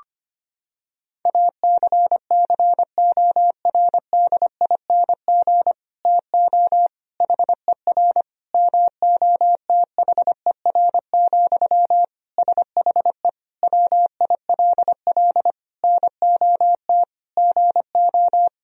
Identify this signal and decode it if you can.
Morse code, heard as a keyed tone, about 25 wpm, 700 Hz. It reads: ACCORDING TO HER MOTHER, SHE WILL NOT GO
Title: Sentences from Top 1000 Words - Review Part 4 - 25wpm